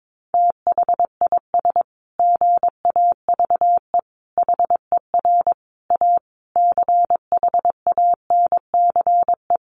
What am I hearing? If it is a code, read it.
Morse code, heard as a keyed tone, about 22 wpm, 700 Hz. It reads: THIS GAVE HER A CHANCE
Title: Sentences from Top 1400 Words 22wpm